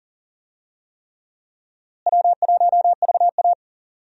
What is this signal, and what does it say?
Morse code, heard as a keyed tone, about 40 wpm, 700 Hz. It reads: W1VA
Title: USA Call Signs - 4 Characters - Repeated 3x - 40wpm